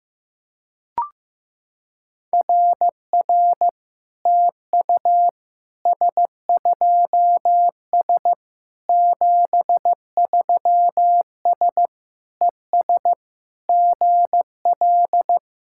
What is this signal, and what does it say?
Morse code, heard as a keyed tone, about 16 wpm, 700 Hz. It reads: RR TU S2S 73S ES GL
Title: SOTA QSO Practice 15wpm